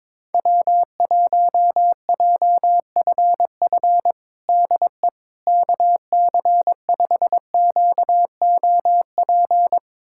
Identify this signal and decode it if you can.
Morse code, heard as a keyed tone, about 22 wpm, 700 Hz. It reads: W1JFF DE KC5QOP
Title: USA Callsign DE Callsign 22wpm